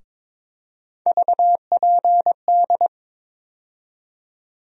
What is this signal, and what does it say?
Morse code, heard as a keyed tone, about 22 wpm, 700 Hz. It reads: VPD
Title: USA Call Sign Postfixes - 3 Letters 22wpm